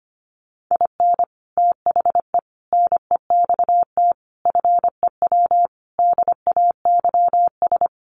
Morse code, heard as a keyed tone, about 25 wpm, 700 Hz. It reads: IN THE NEXT FEW DAYS
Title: Sets of 5 Words Encore2 25wpm